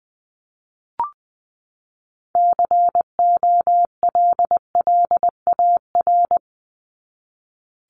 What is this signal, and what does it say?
Morse code, heard as a keyed tone, about 20 wpm, 700 Hz. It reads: COLLAR
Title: Top 1751-2000 Words 20wpm